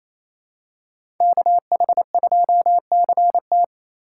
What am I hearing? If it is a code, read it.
Morse code, heard as a keyed tone, about 28 wpm, 700 Hz. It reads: KH2CT
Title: USA Call Signs - 5 Characters - 4x Character Spacing - 28wpm